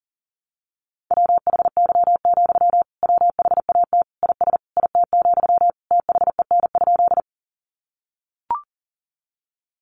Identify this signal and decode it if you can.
Morse code, heard as a keyed tone, about 40 wpm, 700 Hz. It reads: WHY, WHAT IS IT, THEN?
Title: Sentences from Top 200 Words 40wpm